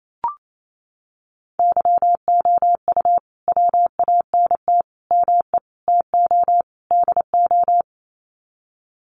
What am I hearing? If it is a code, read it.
Morse code, heard as a keyed tone, about 28 wpm, 700 Hz. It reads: YOU WANT ME TO DO
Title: Sets of 5 Words Encore2 28wpm